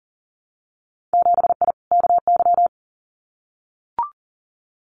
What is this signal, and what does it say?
Morse code, heard as a keyed tone, about 41 wpm, 700 Hz. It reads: 7I KY
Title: Winter Field Day Exchanges Encore 40wpm